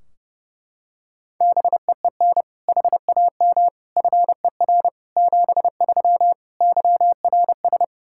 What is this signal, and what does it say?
Morse code, heard as a keyed tone, about 32 wpm, 700 Hz. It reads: BEEN HAM FER 73 YRS
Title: K1USN SST QSO Practice 30wpm